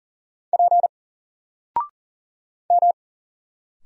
Morse code, heard as a keyed tone, about 40 wpm, 700 Hz. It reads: P M
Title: Single Letter-Number 40wpm